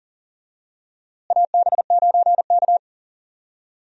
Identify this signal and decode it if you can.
Morse code, heard as a keyed tone, about 40 wpm, 700 Hz. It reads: AB9K
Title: USA Call Signs - 4 Characters - Repeated 2x - 40wpm